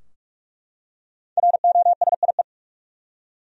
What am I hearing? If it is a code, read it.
Morse code, heard as a keyed tone, about 43 wpm, 700 Hz. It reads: ROSIE
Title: USA Common Names - 45wpm